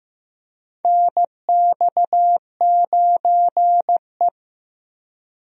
Morse code, heard as a keyed tone, about 15 wpm, 700 Hz. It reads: NX9E
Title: USA Call Signs - 4 Characters - Repeated 3x - 15wpm